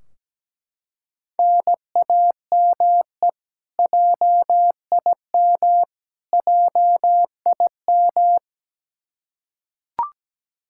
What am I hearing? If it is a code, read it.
Morse code, heard as a keyed tone, about 17 wpm, 700 Hz. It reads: NAME JIM JIM
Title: Practice QSO - Exchange 1 - Beginning - 17wpm